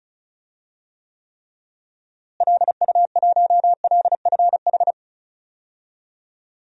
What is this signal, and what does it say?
Morse code, heard as a keyed tone, about 34 wpm, 700 Hz. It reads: LU1LFH
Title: DXCC-100 Call Signs - 6 Characters - Repeated 3x - 35wpm